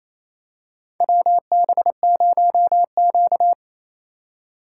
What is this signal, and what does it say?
Morse code, heard as a keyed tone, about 28 wpm, 700 Hz. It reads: WB0Q
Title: USA Call Signs - 4 Characters - 2x Character Spacing - 28wpm